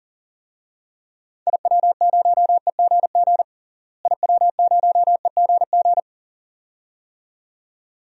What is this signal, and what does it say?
Morse code, heard as a keyed tone, about 39 wpm, 700 Hz. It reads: IW0EGG IW0EGG
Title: DXCC-100 Call Signs - 6 Characters - Repeated 2x - 40wpm